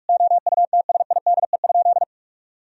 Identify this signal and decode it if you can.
Morse code, heard as a keyed tone, about 46 wpm, 700 Hz. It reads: OUTSIDE?
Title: Sentences from Top 1000 Words - Review Part 4 - 45wpm